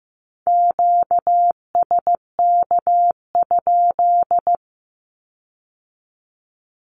Morse code, heard as a keyed tone, about 15 wpm, 700 Hz. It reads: QSK?
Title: Q-Codes with Spoken Meaning - 15wpm